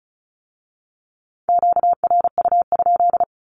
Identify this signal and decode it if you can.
Morse code, heard as a keyed tone, about 36 wpm, 700 Hz. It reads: QRU?
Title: Q-Codes with Spoken Meaning - 35wpm